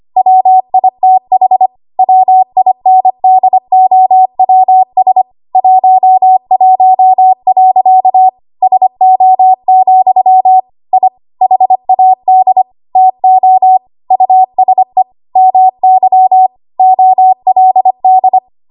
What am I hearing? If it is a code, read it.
Morse code, heard as a keyed tone, about 25 wpm, 750 Hz. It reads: WITH WINDOWS 11. SO, I HAD TO USE MY OLD
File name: Archive / 25 WPM / 25WPM